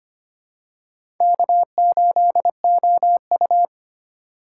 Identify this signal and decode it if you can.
Morse code, heard as a keyed tone, about 25 wpm, 700 Hz. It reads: K8OU
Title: USA Call Signs - 4 Characters - 2x Character Spacing - 25wpm